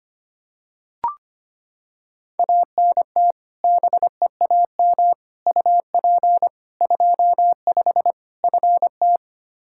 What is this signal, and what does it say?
Morse code, heard as a keyed tone, about 25 wpm, 700 Hz. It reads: ANT BEAM UP 25 FT
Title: K1USN SST QSO Practice 25wpm